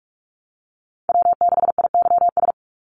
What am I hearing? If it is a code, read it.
Morse code, heard as a keyed tone, about 44 wpm, 700 Hz. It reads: W6IYS
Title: Call Signs Repeated 2x - 45wpm